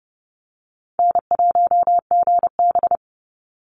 Morse code, heard as a keyed tone, about 30 wpm, 700 Hz. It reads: N1GB
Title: NA SOTA Chaser Callsigns 30wpm